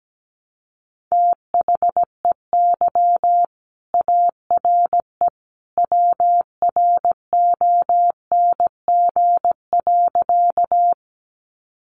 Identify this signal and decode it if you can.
Morse code, heard as a keyed tone, about 17 wpm, 700 Hz. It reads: THEY ARE WRONG.
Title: Sentences from Top 900 Words 17wpm